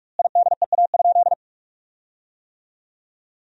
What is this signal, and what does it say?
Morse code, heard as a keyed tone, about 41 wpm, 700 Hz. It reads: IDEA?
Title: 4x Word-Spacing - Sentences from Top 400 Words - 45wpm